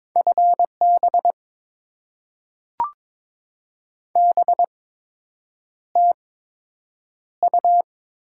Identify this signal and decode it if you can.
Morse code, heard as a keyed tone, about 22 wpm, spaced as for 8 wpm, 700 Hz. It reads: FB BTU
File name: Top 20 QSO Elements - 8x Character Spacing - 22wpm